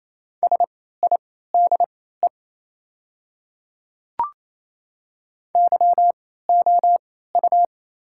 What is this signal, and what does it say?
Morse code, heard as a keyed tone, about 28 wpm, spaced as for 16 wpm, 700 Hz. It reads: SIDE YOU
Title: 3x Character Spacing - Top 100 Words - 28wpm